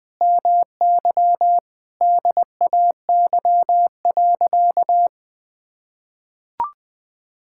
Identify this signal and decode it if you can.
Morse code, heard as a keyed tone, about 20 wpm, 700 Hz. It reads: MY DAY.
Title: Sentences with 9-Word Phrases - Spoken Beginning - 20wpm